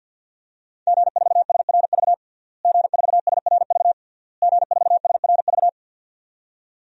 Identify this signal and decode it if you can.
Morse code, heard as a keyed tone, about 52 wpm, 700 Hz. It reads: G4SRV G4SRV G4SRV
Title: DXCC-100 Call Signs - 5 Characters - Repeated 3x - 50wpm